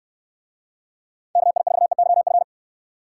Morse code, heard as a keyed tone, about 69 wpm, 700 Hz. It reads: KI4EYF